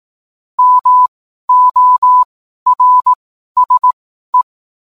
Morse code, heard as a keyed tone, about 18 wpm, spaced as for 12 wpm, 1000 Hz. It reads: MORSE
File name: cwblog / cw mp3 directory / mem morse 12